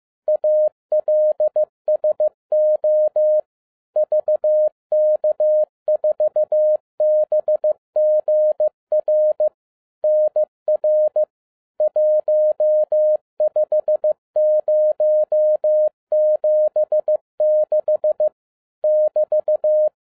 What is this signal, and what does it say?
Morse code, heard as a keyed tone, about 15 wpm, 610 Hz. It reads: ALSO VK4BGR NR 15076 <BT>